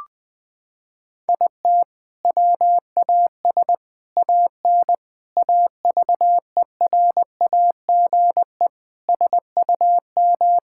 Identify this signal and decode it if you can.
Morse code, heard as a keyed tone, about 20 wpm, 700 Hz. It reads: IT WAS AN AVERAGE SUM
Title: Sentences from Top 1000 Words - Review Part 6 - 20wpm